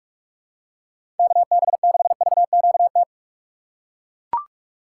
Morse code, heard as a keyed tone, about 43 wpm, 700 Hz. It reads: KB6VQT
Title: USA Call Signs - 6 Characters - 4x Character Spacing - 45wpm